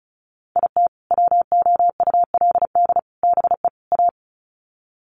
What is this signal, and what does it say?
Morse code, heard as a keyed tone, about 34 wpm, 700 Hz. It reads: IT WOULD BE A
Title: Sets of 4 Words 35wpm